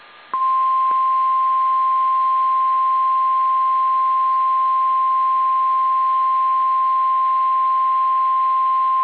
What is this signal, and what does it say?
Emergency Alert System (SAME) tones (the 1050 Hz attention tone). It is heard from 1.1 to 8.0 s.
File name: eas / KSDZKDJL / KSDZKDJL